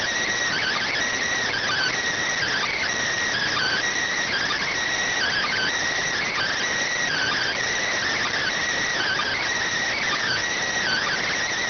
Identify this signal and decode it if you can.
Emergency Alert System (SAME) tones (header bursts), heard as harsh buzzing bursts. It is heard from 0.4 to 11.1 s.